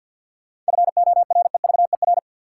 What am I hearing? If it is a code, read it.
Morse code, heard as a keyed tone, about 52 wpm, 700 Hz. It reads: FOREVER
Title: Sentences from Top 1600 Words - New Word Drill Alt - 50wpm